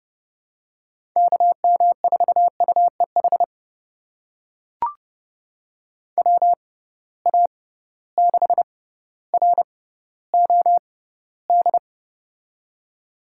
Morse code, heard as a keyed tone, about 30 wpm, spaced as for 15 wpm, 700 Hz. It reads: KM4UEH WA6ROD
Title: USA Call Signs - 6 Characters - 6x Character Spacing - 30wpm